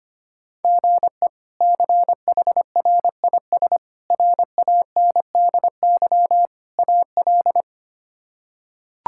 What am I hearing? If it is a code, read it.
Morse code, heard as a keyed tone, about 25 wpm, 700 Hz. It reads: GE CHRIS RANDY AL
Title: K1USN SST QSO Practice - 25wpm